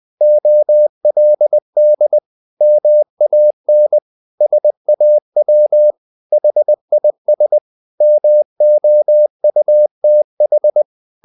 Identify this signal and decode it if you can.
Morse code, heard as a keyed tone, about 20 wpm, 600 Hz. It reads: OLD MAN SAW HIS MOUTH
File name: the old man and the sea part 2 20wpm